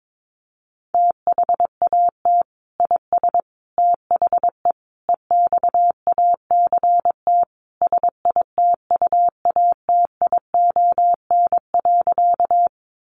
Morse code, heard as a keyed tone, about 22 wpm, 700 Hz. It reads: THAT IS THE EXACT SITUATION.
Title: Sentences from Top 1300 Words 22wpm